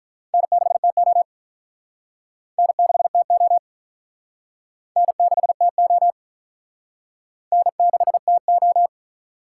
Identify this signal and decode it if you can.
Morse code, heard as a keyed tone, about 43 wpm, 700 Hz. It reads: N6TO N6TO N6TO N6TO
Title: Speed Racing - USA 1x2 Call Signs 35wpm